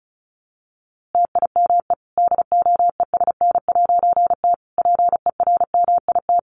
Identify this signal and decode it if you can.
Morse code, heard as a keyed tone, about 34 wpm, 700 Hz. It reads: TIME DOESN'T PERMIT
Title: A Sound Of Thunder 35wpm